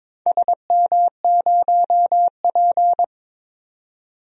Morse code, heard as a keyed tone, about 22 wpm, 700 Hz. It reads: SM0P
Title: DXCC-100 Call Signs - 4 Characters - 22wpm